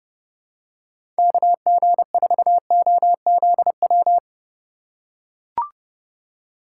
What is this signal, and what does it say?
Morse code, heard as a keyed tone, about 30 wpm, 700 Hz. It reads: KG4OZW
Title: USA Call Signs - 6 Characters - 4x Character Spacing - 30wpm